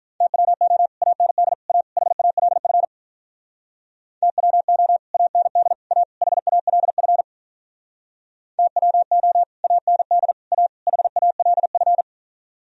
Morse code, heard as a keyed tone, about 45 wpm, 700 Hz. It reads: TWO AND A HALF TWO AND A HALF TWO AND A HALF